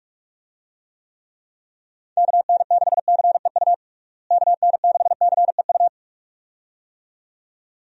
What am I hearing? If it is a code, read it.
Morse code, heard as a keyed tone, about 46 wpm, 700 Hz. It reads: KN6CEU KN6CEU
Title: USA Call Signs - 6 Characters - Repeated 2x - 45wpm